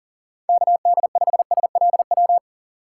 Morse code, heard as a keyed tone, about 40 wpm, 700 Hz. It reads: KD5SLW
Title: USA Call Signs - 6 Characters - 4x Character Spacing - 40wpm